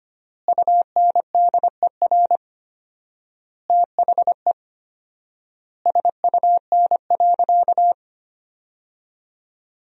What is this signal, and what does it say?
Morse code, heard as a keyed tone, about 25 wpm, 700 Hz. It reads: UNDER THE SUN.
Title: 4x Word-Spacing - Sentences from Top 500 Words - 25wpm